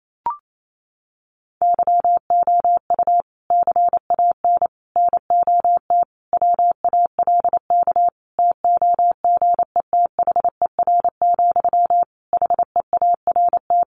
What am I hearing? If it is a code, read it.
Morse code, heard as a keyed tone, about 28 wpm, 700 Hz. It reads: YOU CAN NOT WALK TOGETHER, HEART
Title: Sentences from Top 1500 Words 28wpm